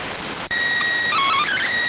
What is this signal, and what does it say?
SSTV (slow-scan television), PD 290.